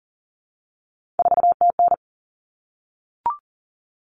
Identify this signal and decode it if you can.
Morse code, heard as a keyed tone, about 40 wpm, 700 Hz. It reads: 4TN
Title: All Cut Numbers - 3-Digit 40wpm